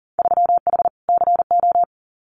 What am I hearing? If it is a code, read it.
Morse code, heard as a keyed tone, about 42 wpm, 700 Hz. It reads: 3H CO
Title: Winter Field Day Exchanges 40wpm